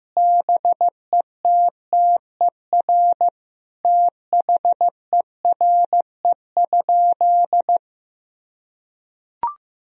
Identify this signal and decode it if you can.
Morse code, heard as a keyed tone, about 15 wpm, 700 Hz. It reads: BETTER THERE?
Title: Sentences from Top 1000 Words - Review Part 4 - 15wpm